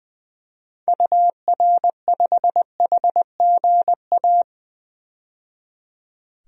Morse code, heard as a keyed tone, about 20 wpm, 700 Hz. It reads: UR5HGA
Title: DXCC-100 Call Signs - 6 Characters - 20wpm